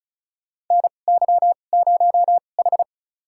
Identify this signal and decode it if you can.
Morse code, heard as a keyed tone, about 36 wpm, spaced as for 25 wpm, 690 Hz. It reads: NY0H
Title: USA Call Signs - 4 Characters - 2x Character Spacing - 35wpm